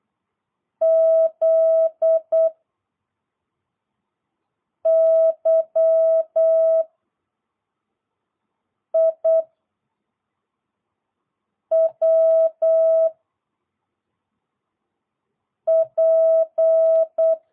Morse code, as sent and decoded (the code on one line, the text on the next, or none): --.. / -.-- / .. / .-- / .--.
Z Y I W P